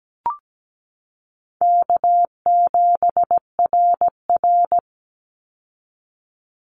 Morse code, heard as a keyed tone, about 17 wpm, 700 Hz. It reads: K7RR
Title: Easy Call Signs - 17wpm